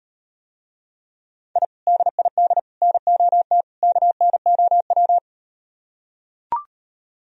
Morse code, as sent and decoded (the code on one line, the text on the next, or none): .. / -.. .. -.. / -. --- - / -.- -. --- .--
I DID NOT KNOW